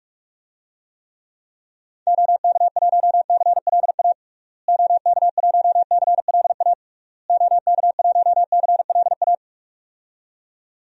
Morse code, heard as a keyed tone, about 42 wpm, 700 Hz. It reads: OK1CLA OK1CLA OK1CLA
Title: DXCC-100 Call Signs - 6 Characters - Repeated 3x - 45wpm